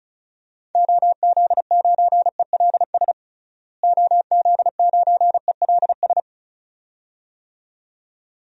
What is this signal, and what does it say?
Morse code, heard as a keyed tone, about 36 wpm, 700 Hz. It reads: OZ9ELS OZ9ELS
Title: DXCC-100 Call Signs - 6 Characters - Repeated 2x - 35wpm